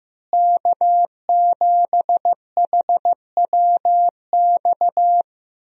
Morse code, heard as a keyed tone, about 16 wpm, 700 Hz. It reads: K7HWX
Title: USA Call Signs - 5 Characters - Repeated 3x - 15wpm